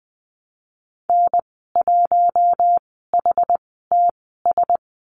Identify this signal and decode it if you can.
Morse code, heard as a keyed tone, about 20 wpm, spaced as for 14 wpm, 700 Hz. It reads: N1HTS
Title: 154 Lesson 25g - Letter p - Call Signs 20wpm